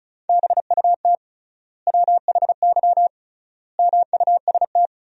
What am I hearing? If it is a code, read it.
Morse code, heard as a keyed tone, about 36 wpm, 700 Hz. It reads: BUT WHY MUST
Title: 3x Word-Spacing - Sentences from Top 400 Words - 35wpm